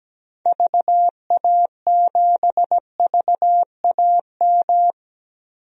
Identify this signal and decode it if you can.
Morse code, heard as a keyed tone, about 17 wpm, 700 Hz. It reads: VA7VAM
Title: DXCC-100 Call Signs - 6 Characters - Repeated 2x - 17wpm